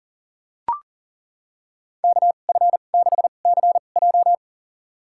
Morse code, heard as a keyed tone, about 40 wpm, spaced as for 29 wpm, 700 Hz. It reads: KF6CJ
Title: USA Call Signs - 5 Characters - 2x Character Spacing - 40wpm